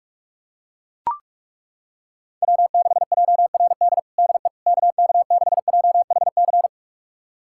Morse code, heard as a keyed tone, about 41 wpm, 700 Hz. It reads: W6JRD DE KK6JHC
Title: USA Callsign DE Callsign 45wpm